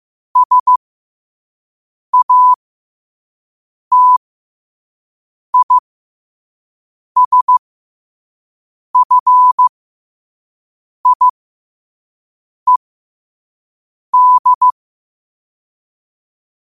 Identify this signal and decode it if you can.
Morse code, heard as a keyed tone, about 15 wpm, spaced as for 5 wpm, 1000 Hz. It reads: SATISFIED
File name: qotdfiles / 5wpm